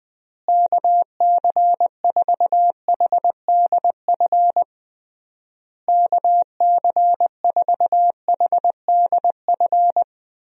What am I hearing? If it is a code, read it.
Morse code, heard as a keyed tone, about 20 wpm, 700 Hz. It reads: KC4HDF KC4HDF
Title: USA Call Signs - 6 Characters - Repeated 3x - 20wpm